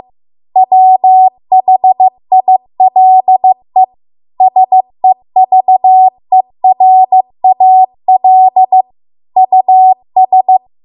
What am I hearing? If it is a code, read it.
Morse code, heard as a keyed tone, about 15 wpm, 750 Hz. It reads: WHILE SEVERAL US